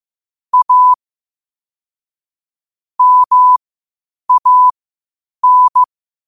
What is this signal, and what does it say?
Morse code, heard as a keyed tone, about 15 wpm, spaced as for 8 wpm, 1000 Hz. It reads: A MAN
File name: qotdfiles / 8wpm